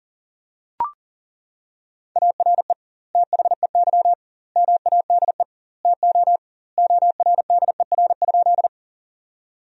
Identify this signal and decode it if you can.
Morse code, heard as a keyed tone, about 40 wpm, 700 Hz. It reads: ARE THEY MADE TO ORDER?
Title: 2x Word-Spacing - Sentences from Top 400 Words - 40wpm